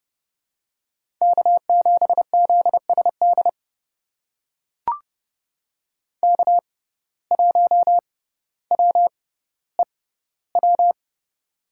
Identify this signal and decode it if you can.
Morse code, heard as a keyed tone, about 30 wpm, spaced as for 15 wpm, 700 Hz. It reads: K7ZSD K1WEW
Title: USA Call Signs - 5 Characters - 6x Character Spacing - 30wpm